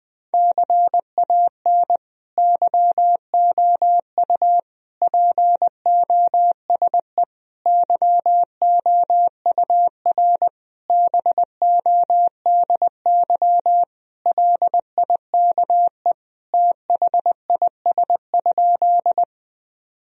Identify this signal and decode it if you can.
Morse code, heard as a keyed tone, about 21 wpm, 700 Hz. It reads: CAN YOU POSE YOUR BODY LIKE THIS?
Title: Sentences from Top 1000 Words - Review Part 2 - 20wpm